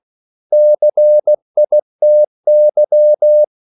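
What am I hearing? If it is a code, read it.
Morse code, heard as a keyed tone, about 16 wpm, 600 Hz. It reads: CITY
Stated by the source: CW practice of national capitals at 16 words per minute, from ON6ZQ.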